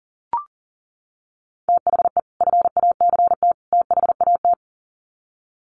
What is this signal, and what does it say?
Morse code, heard as a keyed tone, about 40 wpm, 700 Hz. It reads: THE FACT THAT
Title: Sets of 3 Words 40wpm